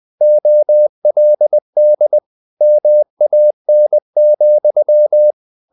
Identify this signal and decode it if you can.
Morse code, heard as a keyed tone, about 20 wpm, 600 Hz. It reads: OLD MAN,
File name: the old man and the sea part 2 20wpm